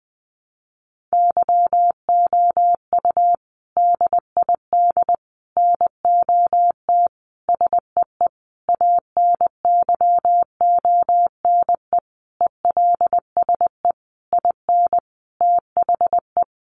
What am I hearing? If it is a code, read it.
Morse code, heard as a keyed tone, about 20 wpm, 700 Hz. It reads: YOU DID NOT SEE ANYONE ELSE IN THE
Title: Sentences from Top 900 Words 20wpm